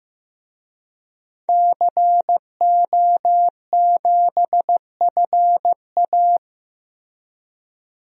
Morse code, heard as a keyed tone, about 15 wpm, 700 Hz. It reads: CO7FA
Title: DXCC-100 Call Signs - 5 Characters - 15wpm